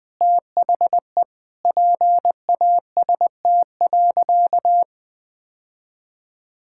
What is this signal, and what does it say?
Morse code, heard as a keyed tone, about 20 wpm, 700 Hz. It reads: THE PAST.
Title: Sentences from Top 1000 Words - Review Part 6 - 20wpm